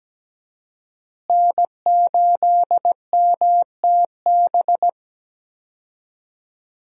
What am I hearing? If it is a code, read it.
Morse code, heard as a keyed tone, about 17 wpm, 700 Hz. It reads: N8MTB
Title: USA Call Signs - 5 Characters - Repeated 3x - 17wpm